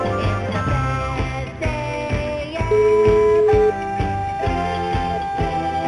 Ringback tone (Europe, 425 Hz), with noise.